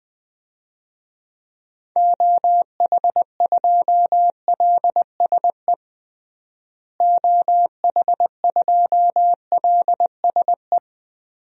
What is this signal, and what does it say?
Morse code, heard as a keyed tone, about 20 wpm, 700 Hz. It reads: OH2LSE OH2LSE
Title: DXCC-100 Call Signs - 6 Characters - Repeated 3x - 20wpm